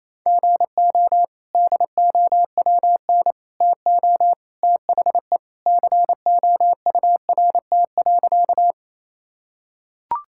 Morse code, heard as a keyed tone, about 28 wpm, 700 Hz. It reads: GO DOWN TO THE COURT.
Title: Sentences from Top 1600 Words 28wpm